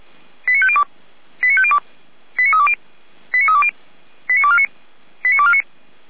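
Five-tone selective call (ZVEI), heard as quick runs of five tones, tones of about 70 ms. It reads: 89502 89502 89320 89320 89249 89249
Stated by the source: ZVEI five-tone selective call received on an SDR, from feuerwehr-orlamuende.org (fire brigade Orlamünde).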